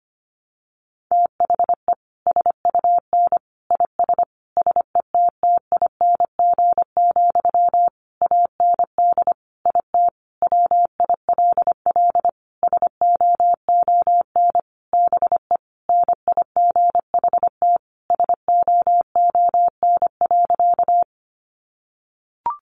Morse code, heard as a keyed tone, about 25 wpm, 700 Hz. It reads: THE SUN IS SETTING, AND IT WILL SOON BE NIGHT SOON.
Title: Sentences from Top 1300 Words 25wpm